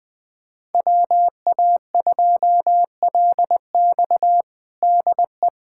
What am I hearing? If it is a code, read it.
Morse code, heard as a keyed tone, about 20 wpm, 700 Hz. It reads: WA2LX DE
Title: USA Callsign DE Callsign - 5 Characters - 20wpm